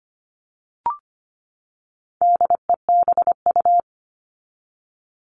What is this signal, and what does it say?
Morse code, heard as a keyed tone, about 25 wpm, 700 Hz. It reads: DEBU
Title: All Cut Numbers - 4-Digit 25wpm